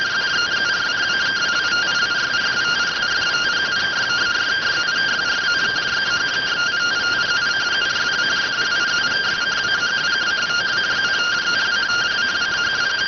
Two-tone teleprinter signal, about 170 Hz shift.